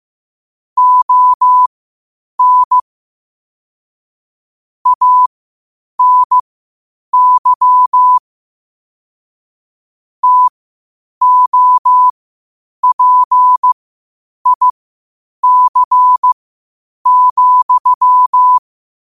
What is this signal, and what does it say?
Morse code, heard as a keyed tone, about 15 wpm, spaced as for 8 wpm, 1000 Hz. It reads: ON ANY TOPIC,
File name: qotdfiles / 8wpm